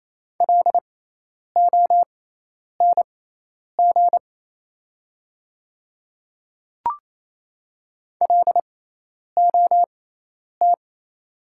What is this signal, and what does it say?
Morse code, heard as a keyed tone, about 27 wpm, spaced as for 10 wpm, 690 Hz. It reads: LONG LOT
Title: 6x Character Spacing - Top 500 Words - 28wpm